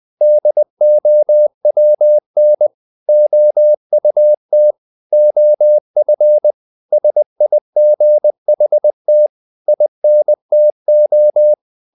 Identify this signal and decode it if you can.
Morse code, heard as a keyed tone, about 20 wpm, 600 Hz. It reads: DOWN OUT OF SIGHT INTO
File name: the old man and the sea / the old man and the sea part 1 20wpm